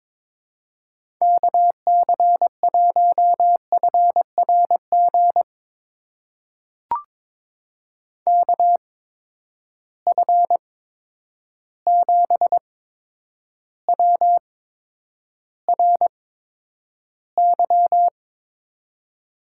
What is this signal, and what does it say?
Morse code, heard as a keyed tone, about 22 wpm, spaced as for 9 wpm, 700 Hz. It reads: KC1FRG KF7WRY
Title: USA Call Signs - 6 Characters - 8x Character Spacing - 22wpm